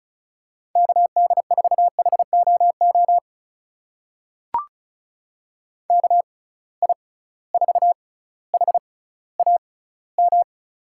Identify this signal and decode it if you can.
Morse code, heard as a keyed tone, about 36 wpm, spaced as for 18 wpm, 700 Hz. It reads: KD4HOO KI4HAM
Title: USA Call Signs - 6 Characters - 6x Character Spacing - 35wpm